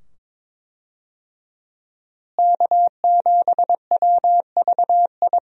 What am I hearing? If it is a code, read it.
Morse code, heard as a keyed tone, about 22 wpm, 700 Hz. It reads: K7WVI
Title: USA Call Signs - 5 Characters - Repeated 2x - 22wpm